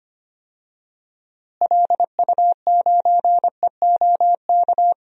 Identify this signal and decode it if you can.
Morse code, heard as a keyed tone, about 25 wpm, 700 Hz. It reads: LU9EOK
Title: DXCC-100 Call Signs - 6 Characters - Repeated 2x - 25wpm